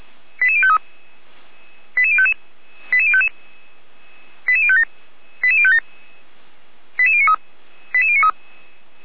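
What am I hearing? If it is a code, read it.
Five-tone selective call (ZVEI), heard as quick runs of five tones, tones of about 70 ms. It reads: 88053 88055 88057 88057 88093 88093
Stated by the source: ZVEI five-tone call 88019 received on an SDR, from the Feuerwehr Orlamünde (fire brigade) server.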